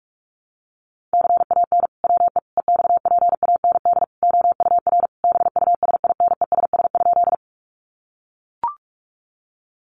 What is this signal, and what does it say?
Morse code, heard as a keyed tone, about 44 wpm, 700 Hz. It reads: CAN WE EXPAND OUR BUSINESS?
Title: Sentences from Top 1400 Words 45wpm